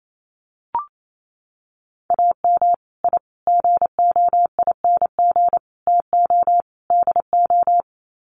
Morse code, heard as a keyed tone, about 28 wpm, 700 Hz. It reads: AM I GOING TO DO
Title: Sets of 5 Words Encore 28wpm